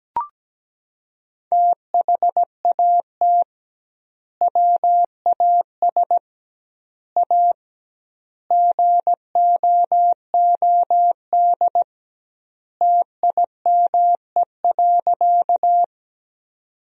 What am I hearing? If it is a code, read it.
Morse code, heard as a keyed tone, about 17 wpm, 700 Hz. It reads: THAT WAS A GOOD TIME.
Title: 2x Word-Spacing - Sentences from Top 200 Words - 17wpm